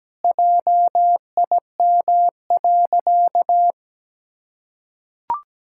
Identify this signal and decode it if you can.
Morse code, heard as a keyed tone, about 17 wpm, 700 Hz. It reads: JIM.
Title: Pangram Challenge v2 - 17wpm